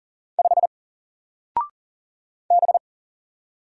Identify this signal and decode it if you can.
Morse code, heard as a keyed tone, about 40 wpm, 700 Hz. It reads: 5 B
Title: Single Letter-Number 40wpm